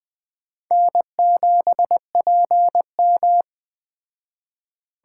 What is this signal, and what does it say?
Morse code, heard as a keyed tone, about 20 wpm, 700 Hz. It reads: N7PM
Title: Speed Racing - USA 1x2 Call Signs 17wpm